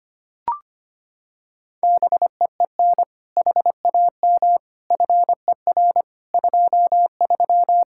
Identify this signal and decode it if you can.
Morse code, heard as a keyed tone, about 25 wpm, 700 Hz. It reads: BEEN HAM FER 23
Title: Practice QSO - Exchange 3 - More Info - 25wpm